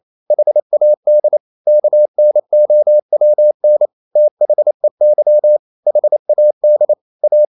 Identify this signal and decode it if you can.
Morse code, heard as a keyed tone, about 28 wpm, 600 Hz. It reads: HAD KNOWN THEY HAD A
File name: Never 0027wpm